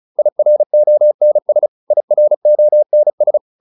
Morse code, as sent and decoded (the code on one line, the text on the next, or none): .. .-. --- -. ... / .. .-. --- -. ...
IRONS IRONS